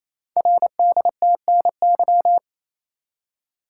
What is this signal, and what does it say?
Morse code, heard as a keyed tone, about 27 wpm, 700 Hz. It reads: RDTNY
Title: Code Groups - 5 Letters - 28wpm